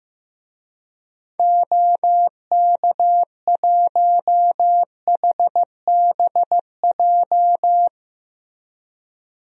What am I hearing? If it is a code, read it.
Morse code, heard as a keyed tone, about 15 wpm, 700 Hz. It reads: OK1HBJ
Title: DXCC-100 Call Signs - 6 Characters - Repeated 3x - 15wpm